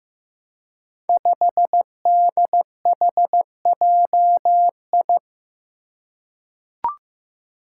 Morse code, heard as a keyed tone, about 15 wpm, 700 Hz. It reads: 5DHJI
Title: Five Challenging Letter-Number Combinations 15wpm